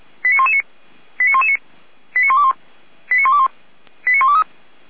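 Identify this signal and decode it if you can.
Five-tone selective call (ZVEI), heard as quick runs of five tones, tones of about 70 ms. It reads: 89119 89119 89121 89121 89123
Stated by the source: Five-tone ZVEI selective call received by SDR, logged by feuerwehr-orlamuende (fire brigade Orlamünde).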